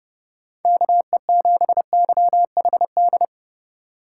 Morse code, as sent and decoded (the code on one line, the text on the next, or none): -.- . --... -.-- .... -..
KE7YHD